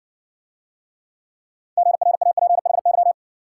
Morse code, heard as a keyed tone, about 56 wpm, 700 Hz. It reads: GRAPHY